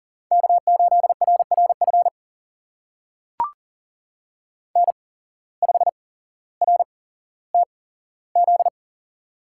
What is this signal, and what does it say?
Morse code, heard as a keyed tone, about 39 wpm, spaced as for 17 wpm, 700 Hz. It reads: K8RRF N5RTZ
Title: USA Call Signs - 5 Characters - 8x Character Spacing - 40wpm